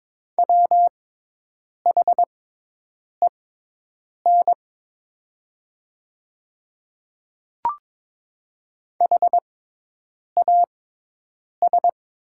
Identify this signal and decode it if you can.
Morse code, heard as a keyed tone, about 22 wpm, spaced as for 8 wpm, 700 Hz. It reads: WHEN HAS
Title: 6x Character Spacing - Top 200 Words - 22wpm